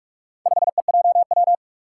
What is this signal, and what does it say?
Morse code, heard as a keyed tone, about 44 wpm, 700 Hz. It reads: 5EJW
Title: Four Challenging Letter-Number Combinations 45wpm